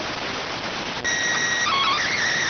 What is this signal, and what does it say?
SSTV (slow-scan television), PD 290.